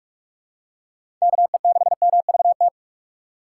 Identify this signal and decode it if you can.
Morse code, heard as a keyed tone, about 42 wpm, 700 Hz. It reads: KE6MVT
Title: USA Call Signs - 6 Characters - 3x Character Spacing - 45wpm